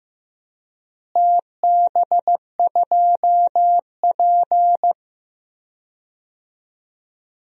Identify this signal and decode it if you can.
Morse code, heard as a keyed tone, about 15 wpm, 700 Hz. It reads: TB2P
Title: DXCC-100 Call Signs - 4 Characters - Repeated 2x - 15wpm